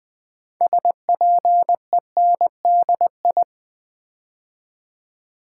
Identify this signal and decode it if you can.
Morse code, heard as a keyed tone, about 21 wpm, 700 Hz. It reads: SPENDI
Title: Build A Word - 2 Letters At A Time - 8-Letter Words - 20wpm